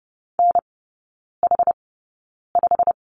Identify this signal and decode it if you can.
Morse code, heard as a keyed tone, about 30 wpm, spaced as for 9 wpm, 700 Hz. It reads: NH5
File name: course / 30:10wpm / 087 Lesson 15e - Number 5 - Three Characters 30wpm